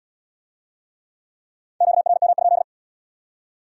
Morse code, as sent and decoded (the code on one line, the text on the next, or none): --.- .-. -- ..--..
QRM?